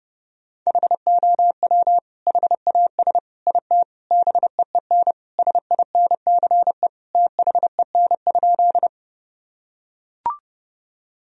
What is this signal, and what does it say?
Morse code, heard as a keyed tone, about 30 wpm, 700 Hz. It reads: HOW HAS IT BEEN SINCE THEN?
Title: Sentences from Top 1000 Words - Review Part 1 - 30wpm